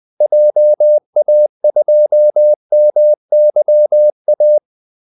Morse code, heard as a keyed tone, about 20 wpm, 600 Hz. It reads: JA2MYA
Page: Kaldesignaler 20 wpm